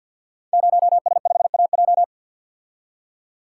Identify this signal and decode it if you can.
Morse code, heard as a keyed tone, about 48 wpm, 700 Hz. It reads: 0S5AJ
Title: Code Groups - 5 Letter:Numbers - 50wpm